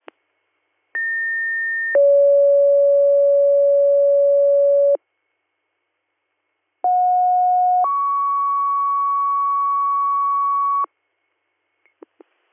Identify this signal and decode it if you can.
Two-tone page, heard as a one-second tone, then a three-second tone, about 1800 then 560 Hz, about 730 then 1120 Hz. It is heard from 0.9 to 10.9 s.